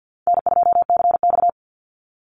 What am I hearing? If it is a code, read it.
Morse code, heard as a keyed tone, about 50 wpm, 700 Hz. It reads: N2CX
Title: Call Signs - 50wpm